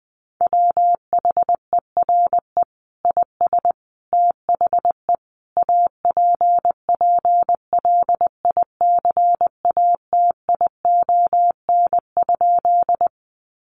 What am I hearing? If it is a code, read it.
Morse code, heard as a keyed tone, about 20 wpm, 700 Hz. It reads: WHERE IS THE APPLICATION?
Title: Sentences from Top 1200 Words 20wpm